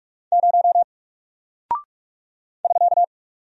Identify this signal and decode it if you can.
Morse code, heard as a keyed tone, about 46 wpm, 700 Hz. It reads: 0 <SK>
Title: Single Character (Letter:Number:Punctuation:Prosign) Alt 45wpm